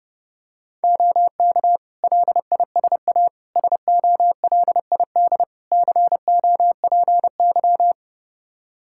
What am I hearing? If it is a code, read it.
Morse code, heard as a keyed tone, about 32 wpm, 700 Hz. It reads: OK LISA SOLID COPY
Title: Practice QSO - Exchange 3 - More Info - 30wpm